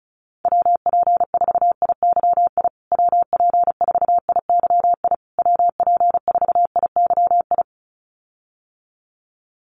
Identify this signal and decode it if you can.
Morse code, heard as a keyed tone, about 34 wpm, 700 Hz. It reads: WP4IYI WP4IYI WP4IYI
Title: USA Call Signs Repeated 3x 35wpm